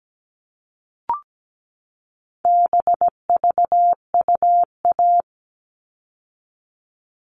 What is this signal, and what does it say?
Morse code, heard as a keyed tone, about 17 wpm, 700 Hz. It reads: BVUA
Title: All Cut Numbers - 4-Digit 17wpm